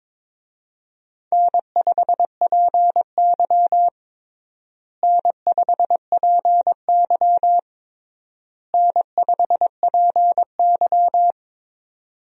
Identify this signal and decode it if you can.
Morse code, heard as a keyed tone, about 22 wpm, 700 Hz. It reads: N5PY N5PY N5PY
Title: USA Call Signs - 4 Characters - Repeated 3x - 22wpm